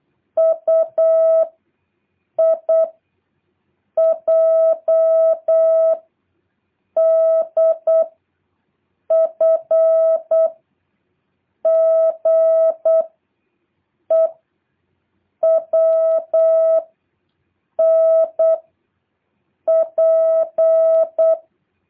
Morse code, heard as a keyed tone, about 8 wpm, 650 Hz. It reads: U I J D F G E W N P
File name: jh8cbh / 2023cw